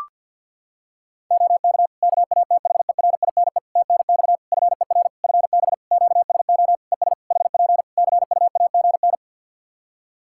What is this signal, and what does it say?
Morse code, heard as a keyed tone, about 52 wpm, 700 Hz. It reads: OK KATHERINE TNX FER FB QSO ES HP CUAGN
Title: K1USN SST QSO Practice 50wpm